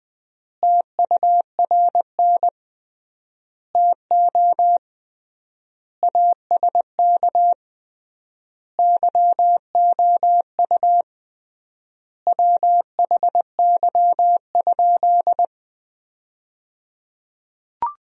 Morse code, heard as a keyed tone, about 21 wpm, 700 Hz. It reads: TURN TO ASK YOU WHY?
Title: 3x Word-Spacing - Sentences from Top 200 Words - 20wpm